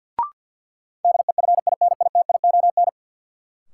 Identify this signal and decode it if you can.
Morse code, heard as a keyed tone, about 50 wpm, 700 Hz. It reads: DEFINITION
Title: Sentences from Top 1000 Words - Review Part 4 - New Word Drill - 50wpm